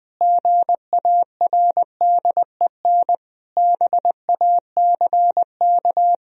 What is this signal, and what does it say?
Morse code, heard as a keyed tone, about 20 wpm, 700 Hz. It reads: GARDEN BACK
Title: Sentences from Top 1000 Words - Review Part 4 - 20wpm